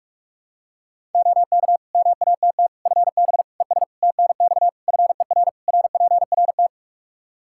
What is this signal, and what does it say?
Morse code, heard as a keyed tone, about 41 wpm, 700 Hz. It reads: OK MATT FB ES TNX FER RPRT
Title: Practice QSO - Exchange 2 - Info - 45wpm